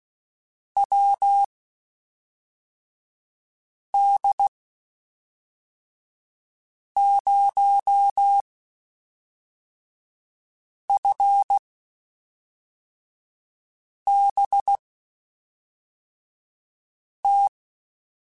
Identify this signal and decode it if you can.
Morse code, heard as a keyed tone, about 16 wpm, spaced as for 3 wpm, 800 Hz. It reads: WD0FBT
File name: (spaced) / QSOs Spaced-mp3s / WD0FBT 1 @5 sp